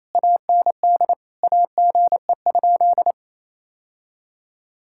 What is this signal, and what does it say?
Morse code, heard as a keyed tone, about 27 wpm, 700 Hz. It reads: AND AGE?
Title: Sentences from Top 1600 Words 28wpm